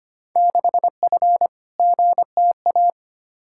Morse code, heard as a keyed tone, about 25 wpm, 700 Hz. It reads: 6F GTA
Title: Field Day Exchanges Encore - 25wpm